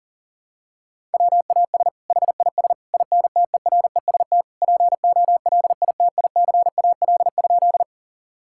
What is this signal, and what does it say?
Morse code, heard as a keyed tone, about 40 wpm, 700 Hz. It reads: WAS HIS INTEREST POLITICAL?
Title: Sentences from Top 1000 Words - Review Part 5 - 40wpm